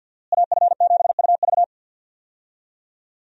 Morse code, heard as a keyed tone, about 48 wpm, 700 Hz. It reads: AF7UV
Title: USA Call Signs - 5 Characters - Repeated 2x - 50wpm